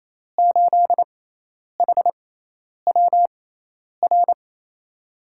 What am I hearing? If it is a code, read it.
Morse code, heard as a keyed tone, about 28 wpm, spaced as for 10 wpm, 700 Hz. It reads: 8HWR
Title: USA Call Signs - 6 Characters - 6x Character Spacing - 28wpm